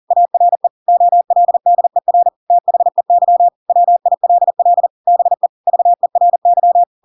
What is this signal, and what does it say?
Morse code, heard as a keyed tone, about 40 wpm, 700 Hz. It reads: ARE OLDER THEY WILL BE VERY
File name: Pinocchio / Pinocchio ch4 40 40 ik1hge